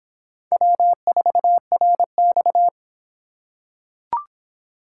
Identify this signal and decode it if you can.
Morse code, heard as a keyed tone, about 26 wpm, 700 Hz. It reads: W4RX